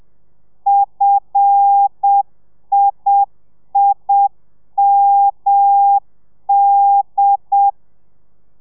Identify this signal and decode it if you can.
Morse code, heard as a keyed tone, about 7 wpm, 800 Hz. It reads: FIIMD
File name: telegrafia / i2tfj / Lez5 30